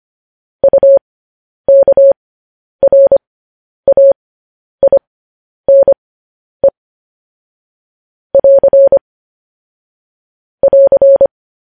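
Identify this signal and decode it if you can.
Morse code, heard as a keyed tone, about 25 wpm, spaced as for 10 wpm, 550 Hz. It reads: UKRAINE <AR> <AR>
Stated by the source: Morse code transmission of news headlines.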